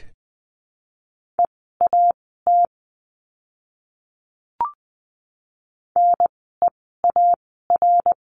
Morse code, heard as a keyed tone, about 20 wpm, spaced as for 14 wpm, 700 Hz. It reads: EAT NEAR
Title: 061 Lesson 11f - Letter h - Words 20wpm